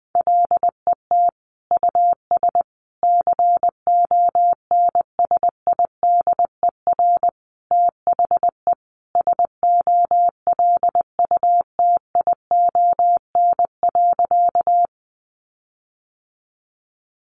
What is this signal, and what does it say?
Morse code, heard as a keyed tone, about 20 wpm, 700 Hz. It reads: LET US CONSIDER THE SOLUTION.
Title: Sentences from Top 1300 Words 20wpm